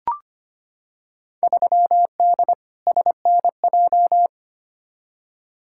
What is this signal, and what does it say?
Morse code, heard as a keyed tone, about 25 wpm, 700 Hz. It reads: 3D SNJ
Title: Field Day Exchanges Encore - 25wpm